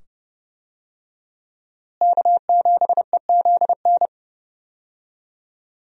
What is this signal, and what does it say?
Morse code, heard as a keyed tone, about 29 wpm, 700 Hz. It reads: K7EZN
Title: USA Call Signs - 5 Characters - Repeated 2x - 30wpm